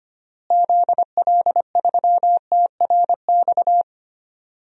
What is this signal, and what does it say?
Morse code, heard as a keyed tone, about 25 wpm, 700 Hz. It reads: ZL3TRX
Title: DXCC-100 Call Signs - 6 Characters - Repeated 3x - 25wpm